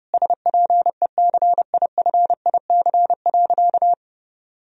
Morse code, heard as a keyed tone, about 32 wpm, 700 Hz. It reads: SPECIFIC.
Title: Sentences from Top 1000 Words - Review Part 3 - 30wpm